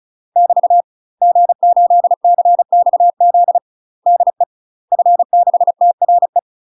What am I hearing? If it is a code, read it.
Morse code, heard as a keyed tone, about 34 wpm, 700 Hz. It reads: <BT> G8CXZ DE F6TRE
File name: UK MASTER QSOs TRAINING / 35WPM / UK MASTER QSOs 0003